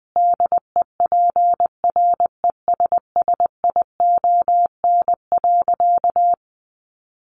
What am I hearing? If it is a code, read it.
Morse code, heard as a keyed tone, about 19 wpm, 700 Hz. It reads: DEPRESSION.
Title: Sentences from Top 1300 Words 20wpm